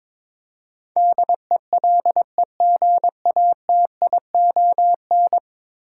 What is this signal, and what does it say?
Morse code, heard as a keyed tone, about 22 wpm, 700 Hz. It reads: DELEGATION
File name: practice / 22wpm / 10 Letter Words - First 3 Letters Given - 22wpm